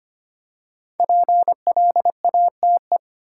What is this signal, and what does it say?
Morse code, heard as a keyed tone, about 25 wpm, 700 Hz. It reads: PLATE
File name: 5 Letter Words - First 3 Letters Given - 25wpm